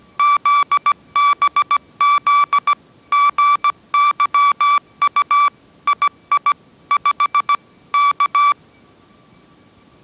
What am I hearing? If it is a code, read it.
Morse code, heard as a keyed tone, about 17 wpm, 1160 Hz. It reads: ZBZ GYU II 5 K
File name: 18wpm